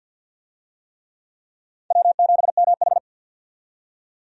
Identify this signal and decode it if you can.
Morse code, heard as a keyed tone, about 47 wpm, 700 Hz. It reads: W7MH
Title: USA Call Signs - 4 Characters - 50wpm